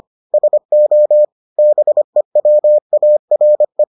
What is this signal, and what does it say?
Morse code, heard as a keyed tone, about 25 wpm, 600 Hz. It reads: SO BEWARE